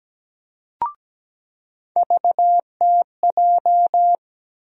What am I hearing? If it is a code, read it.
Morse code, heard as a keyed tone, about 17 wpm, 700 Hz. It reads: VTJ
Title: USA Call Sign Postfixes - 3 Letters 17wpm